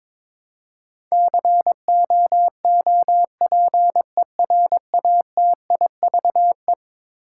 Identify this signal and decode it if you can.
Morse code, heard as a keyed tone, about 22 wpm, 700 Hz. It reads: COOPERATIVE
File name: practice / 22wpm / Eleven Letter Words - 22wpm